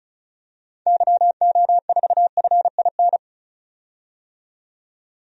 Morse code, heard as a keyed tone, about 34 wpm, 700 Hz. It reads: YO4FIN
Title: DXCC-100 Call Signs - 6 Characters - 35wpm